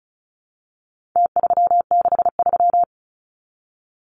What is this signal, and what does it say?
Morse code, heard as a keyed tone, about 34 wpm, 700 Hz. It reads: T363
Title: Numbers - Contest Style - 4-Digit 35wpm